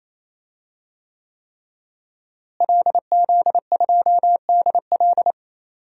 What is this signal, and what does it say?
Morse code, heard as a keyed tone, about 28 wpm, 700 Hz. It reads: LZ2DL
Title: DXCC-100 Call Signs - 5 Characters - Repeated 3x - 28wpm